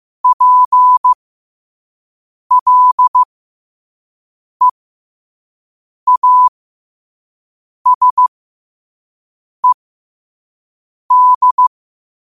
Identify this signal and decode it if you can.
Morse code, heard as a keyed tone, about 15 wpm, spaced as for 5 wpm, 1000 Hz. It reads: PLEASED
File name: qotdfiles / 5wpm